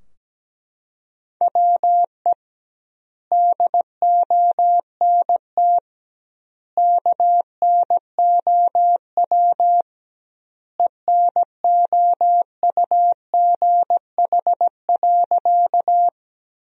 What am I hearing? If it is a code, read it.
Morse code, heard as a keyed tone, about 17 wpm, 700 Hz. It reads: WE DONT KNOW ENOUGH.
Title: 2x Word-Spacing - Sentences from Top 400 Words - 17wpm